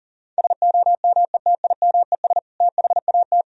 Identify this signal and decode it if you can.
Morse code, heard as a keyed tone, about 40 wpm, 700 Hz. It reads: SOMETIMES THAT
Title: Sentences from Top 1000 Words - Review Part 3 - 40wpm